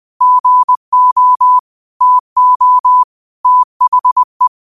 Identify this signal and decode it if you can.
Morse code, heard as a keyed tone, about 20 wpm, 1000 Hz. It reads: GO TO THE